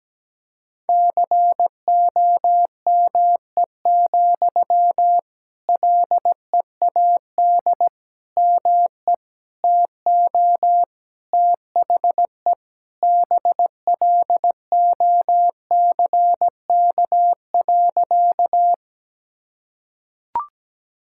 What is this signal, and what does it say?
Morse code, heard as a keyed tone, about 17 wpm, 700 Hz. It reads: COME, LEAD ME TO THE BLOCK.
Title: Sentences from Top 1600 Words 17wpm